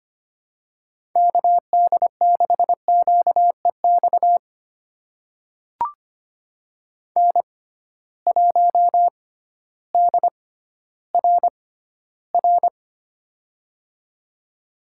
Morse code, heard as a keyed tone, about 25 wpm, spaced as for 14 wpm, 700 Hz. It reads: KD6QEX N1DRR
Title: USA Call Signs - 6x Character Spacing - 25wpm